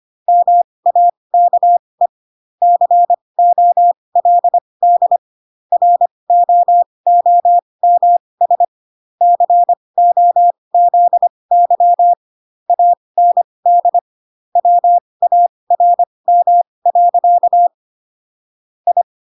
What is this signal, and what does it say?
Morse code, heard as a keyed tone, about 25 wpm, spaced as for 20 wpm, 700 Hz. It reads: MAKE COLD ROOMS COZY AND WARM. I
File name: Pinocchio / Pinocchio ch1 15 25 ik1hge